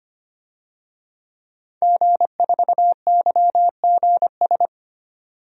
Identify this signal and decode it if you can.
Morse code, heard as a keyed tone, about 25 wpm, 700 Hz. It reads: G4YGS
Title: DXCC-100 Call Signs - 5 Characters - 25wpm